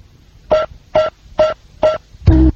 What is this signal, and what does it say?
There is no code.